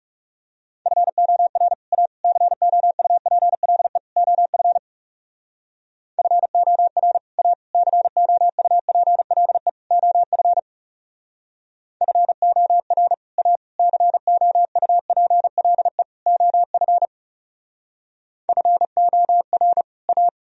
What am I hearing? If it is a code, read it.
Morse code, heard as a keyed tone, about 36 wpm, 700 Hz. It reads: FOR A COUPLE OF FOR A COUPLE OF FOR A COUPLE OF FOR A
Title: Speed-Racing - Sets of 4 Words 30wpm